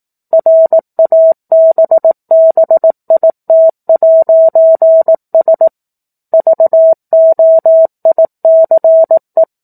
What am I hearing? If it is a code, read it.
Morse code, heard as a keyed tone, about 18 wpm, 650 Hz. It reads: RABBIT'S VOICE
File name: 04-Chapter-04 18 WPM